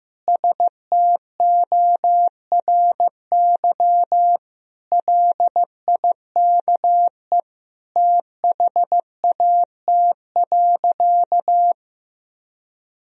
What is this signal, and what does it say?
Morse code, heard as a keyed tone, about 16 wpm, 700 Hz. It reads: STORY LIKE THAT.
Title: Sentences from Top 1000 Words - Review Part 3 - 15wpm